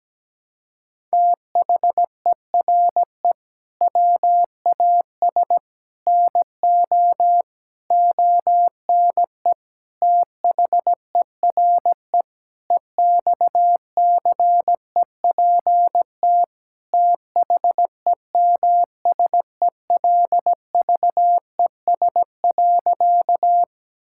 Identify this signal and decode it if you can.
Morse code, heard as a keyed tone, about 17 wpm, 700 Hz. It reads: THERE WAS NO ONE THERE EXCEPT THEMSELVES.
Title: Sentences from Top 1600 Words 17wpm